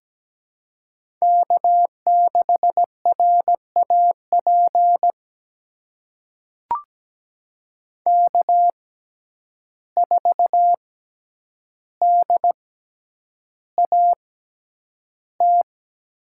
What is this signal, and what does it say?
Morse code, heard as a keyed tone, about 17 wpm, spaced as for 9 wpm, 700 Hz. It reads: K6RAP K4DAT
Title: USA Call Signs - 5 Characters - 6x Character Spacing - 17wpm